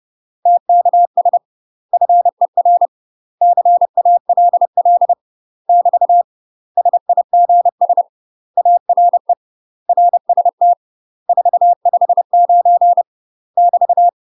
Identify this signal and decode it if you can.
Morse code, heard as a keyed tone, about 30 wpm, 700 Hz. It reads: TKS FER CALL <BT> SIGS ARE RST 459 <BT>
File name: UK MASTER QSOs TRAINING / 30WPM / UK MASTER QSOs 0046